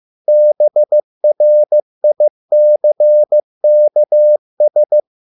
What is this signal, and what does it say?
Morse code, heard as a keyed tone, about 15 wpm, 600 Hz. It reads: BRICKS